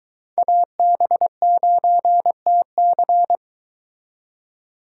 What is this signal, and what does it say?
Morse code, heard as a keyed tone, about 22 wpm, 700 Hz. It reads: AB9TC